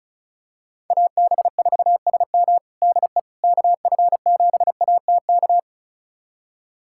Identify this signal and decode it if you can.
Morse code, heard as a keyed tone, about 36 wpm, 700 Hz. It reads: AB4SM DE KF7ATK
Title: USA Callsign DE Callsign 35wpm